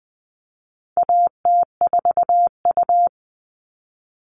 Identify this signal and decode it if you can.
Morse code, heard as a keyed tone, about 20 wpm, 700 Hz. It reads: AT4U
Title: All Cut Numbers - 4-Digit 20wpm